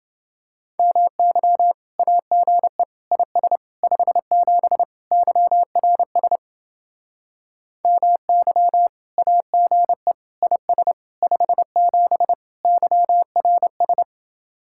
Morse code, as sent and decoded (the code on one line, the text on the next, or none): -- -.-- / .- --. / .. ... / ..... --... / -.-- .-. ... / -- -.-- / .- --. / .. ... / ..... --... / -.-- .-. ...
MY AGE IS 57 YRS MY AGE IS 57 YRS